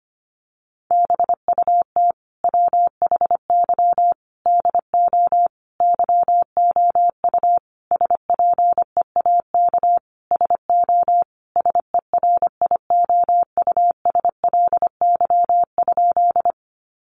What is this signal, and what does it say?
Morse code, heard as a keyed tone, about 25 wpm, 700 Hz. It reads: BUT WHY DO YOU SPEAK SO SERIOUSLY?
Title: Sentences from Top 1200 Words 25wpm